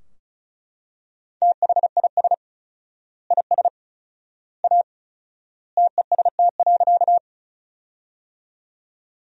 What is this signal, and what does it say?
Morse code, heard as a keyed tone, about 36 wpm, 700 Hz. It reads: THIS IS A TEST.
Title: 4x Word-Spacing - Sentences from Top 500 Words - 35wpm